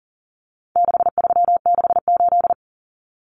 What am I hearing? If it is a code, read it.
Morse code, heard as a keyed tone, about 40 wpm, 700 Hz. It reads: 6368
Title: Common Cut Numbers - 4-Digit 40wpm